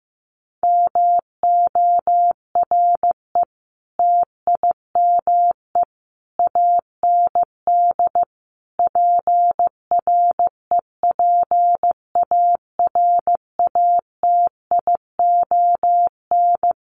Morse code, heard as a keyed tone, about 15 wpm, 700 Hz. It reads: MORE TIME AND PREPARATION
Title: Sentences from Top 1400 Words 15wpm